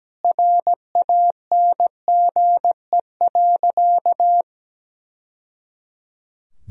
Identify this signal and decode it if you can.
Morse code, heard as a keyed tone, about 17 wpm, 700 Hz. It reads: RANGE.
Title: Sentences from Top 1000 Words - Review Part 4 - 17wpm